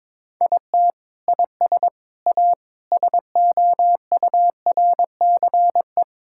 Morse code, heard as a keyed tone, about 22 wpm, 700 Hz. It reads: IT IS A SOURCE
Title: Sentences from Top 1000 Words - Review Part 5 - 22wpm